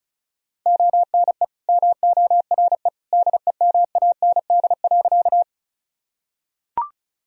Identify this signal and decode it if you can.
Morse code, heard as a keyed tone, about 34 wpm, 700 Hz. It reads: ONE MORE DEMAND.
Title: Sentences from Top 1000 Words - Review Part 4 - 35wpm